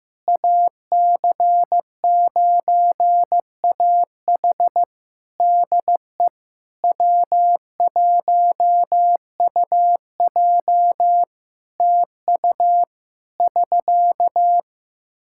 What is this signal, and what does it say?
Morse code, heard as a keyed tone, about 15 wpm, 700 Hz. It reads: AC9AH DE W1UJ TU <SK>
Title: K1USN SST QSO Practice 15wpm